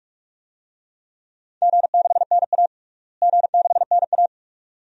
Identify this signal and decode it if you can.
Morse code, heard as a keyed tone, about 46 wpm, 700 Hz. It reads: G6NA G6NA
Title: DXCC-100 Call Signs - 4 Characters - Repeated 2x - 45wpm